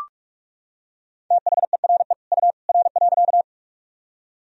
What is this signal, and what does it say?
Morse code, heard as a keyed tone, about 42 wpm, 700 Hz. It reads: THERE U R.
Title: Sentences from Top 100 Words and Abbreviations 45wpm